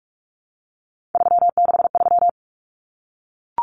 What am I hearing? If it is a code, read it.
Morse code, heard as a keyed tone, about 44 wpm, 700 Hz. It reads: 363
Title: Common Cut Numbers - 3-Digit 45wpm